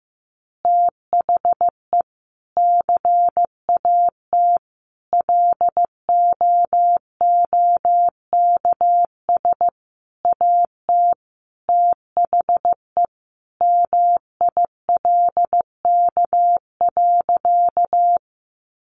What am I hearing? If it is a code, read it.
Morse code, heard as a keyed tone, about 15 wpm, 700 Hz. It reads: THE CAT LOOKS AT THE MILK.
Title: Additional Easy Phrases 15wpm